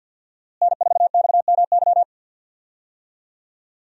Morse code, heard as a keyed tone, about 52 wpm, 700 Hz. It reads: N4XMY
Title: USA Call Signs - 5 Characters - Repeated 3x - 50wpm